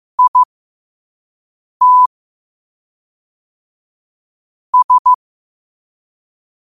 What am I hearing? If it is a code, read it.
Morse code, heard as a keyed tone, about 15 wpm, spaced as for 5 wpm, 1000 Hz. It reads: IT S
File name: qotdfiles / 5wpm